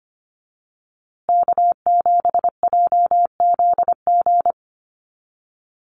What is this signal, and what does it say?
Morse code, heard as a keyed tone, about 25 wpm, 700 Hz. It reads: K7JZG
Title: USA Call Signs Repeated 3x 25wpm